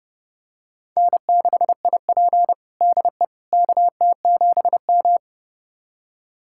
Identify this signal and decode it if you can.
Morse code, heard as a keyed tone, about 30 wpm, 700 Hz. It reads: N6IP DE KT7M
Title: USA Callsign DE Callsign - 4 Characters - 30wpm